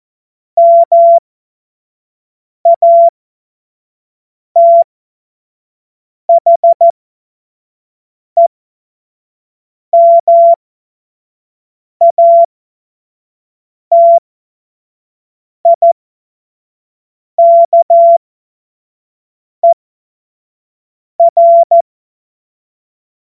Morse code, heard as a keyed tone, about 14 wpm, spaced as for 5 wpm, 680 Hz. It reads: MATHEMATIKER
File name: Morsekurs / Witze / Witz 07. Dezember 2021, 25 F70